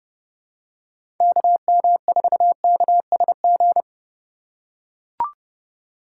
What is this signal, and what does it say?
Morse code, heard as a keyed tone, about 30 wpm, 700 Hz. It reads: KM4KSG
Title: USA Call Signs - 6 Characters - 2x Character Spacing - 30wpm